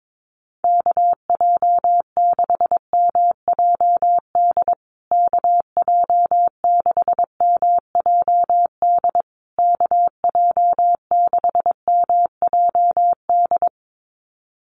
Morse code, heard as a keyed tone, about 22 wpm, 700 Hz. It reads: KJ6MJD KJ6MJD KJ6MJD
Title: USA Call Signs Repeated 3x 22wpm